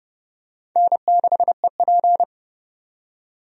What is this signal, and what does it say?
Morse code, heard as a keyed tone, about 30 wpm, 700 Hz. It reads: N6EP
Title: USA Call Signs - 4 Characters - 30wpm